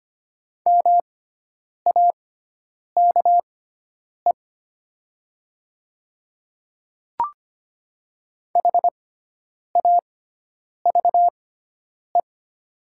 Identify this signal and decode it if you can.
Morse code, heard as a keyed tone, about 25 wpm, spaced as for 9 wpm, 700 Hz. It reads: MAKE HAVE
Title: 6x Character Spacing - Top 200 Words - 25wpm